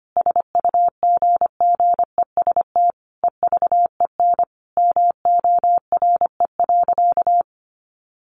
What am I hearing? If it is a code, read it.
Morse code, heard as a keyed tone, about 25 wpm, 700 Hz. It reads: SUGGEST EVEN MORE.
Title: Sentences from Top 1300 Words 25wpm